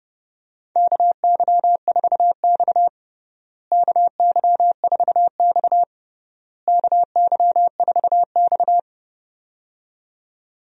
Morse code, heard as a keyed tone, about 30 wpm, 700 Hz. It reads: KY4X KY4X KY4X
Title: USA Call Signs - 4 Characters - Repeated 3x - 30wpm